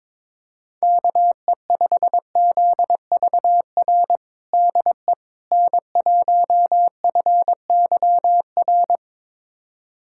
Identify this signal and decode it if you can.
Morse code, heard as a keyed tone, about 22 wpm, 700 Hz. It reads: KE5ZVR DE N1FYR
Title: USA Callsign DE Callsign 22wpm